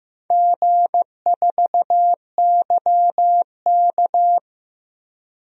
Morse code, heard as a keyed tone, about 15 wpm, 700 Hz. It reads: G4YK
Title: DXCC-100 Call Signs - 4 Characters - 15wpm